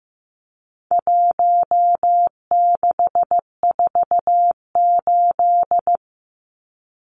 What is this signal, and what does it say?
Morse code, heard as a keyed tone, about 15 wpm, 700 Hz. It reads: 1648
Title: Numbers - 4-Digit 15wpm